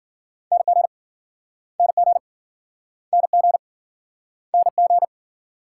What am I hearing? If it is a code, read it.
Morse code, heard as a keyed tone, about 48 wpm, 700 Hz. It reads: NG NG NG NG
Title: Speed-Racing Multi-letter Phonemes - 40wpm